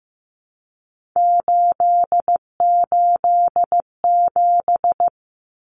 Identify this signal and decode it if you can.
Morse code, heard as a keyed tone, about 15 wpm, 700 Hz. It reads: 887
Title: Common Cut Numbers - 3-Digit 15wpm